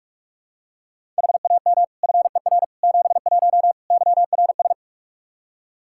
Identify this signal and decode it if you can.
Morse code, heard as a keyed tone, about 42 wpm, 700 Hz. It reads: HAM FER 71 YRS
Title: Practice QSO - Exchange 3 - More Info - 45wpm